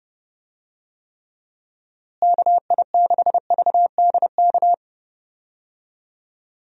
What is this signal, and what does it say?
Morse code, heard as a keyed tone, about 32 wpm, 700 Hz. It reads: KI6VDK
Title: USA Call Signs - 6 Characters - Repeated 2x - 30wpm